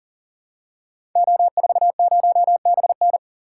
Morse code, heard as a keyed tone, about 40 wpm, 700 Hz. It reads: O40BN
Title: Code Groups - 5 Letter:Numbers - 40wpm